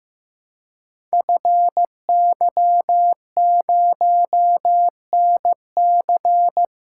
Morse code, heard as a keyed tone, about 15 wpm, 700 Hz. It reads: FY0NC
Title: Code Groups - 5 Letter:Numbers - 15wpm